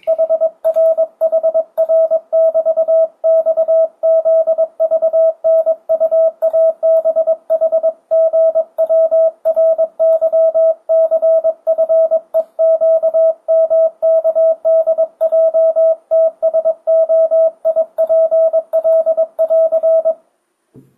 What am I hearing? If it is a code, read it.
Morse code, heard as a keyed tone, about 22 wpm, 650 Hz. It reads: HR HR <BT> XZVNUABHGWRYCFEQMKDJTSOIPL <AR>
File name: 2025cw / speedup / 22wpm